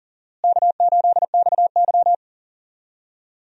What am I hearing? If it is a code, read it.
Morse code, heard as a keyed tone, about 37 wpm, 700 Hz. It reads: K8XY
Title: USA Call Signs - 4 Characters - Repeated 3x - 40wpm